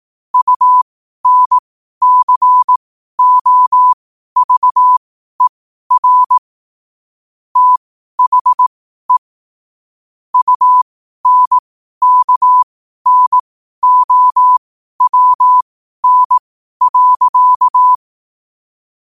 Morse code, heard as a keyed tone, about 18 wpm, spaced as for 12 wpm, 1000 Hz. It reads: UNCOVER THE UNKNOWN.
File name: qotdfiles / 12wpm